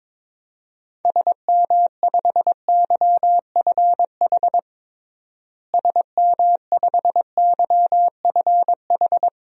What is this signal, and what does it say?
Morse code, heard as a keyed tone, about 21 wpm, 700 Hz. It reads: SM5YFH SM5YFH
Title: DXCC-100 Call Signs - 6 Characters - Repeated 2x - 22wpm